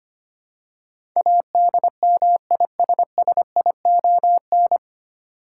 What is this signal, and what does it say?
Morse code, heard as a keyed tone, about 25 wpm, 700 Hz. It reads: ADMISSION
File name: practice / 25wpm / 9 Letter Words - First 3 Letters Given - 25wpm